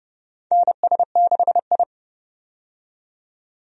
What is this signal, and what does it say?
Morse code, heard as a keyed tone, about 31 wpm, 700 Hz. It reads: NS6I
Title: USA Call Signs - 4 Characters - 30wpm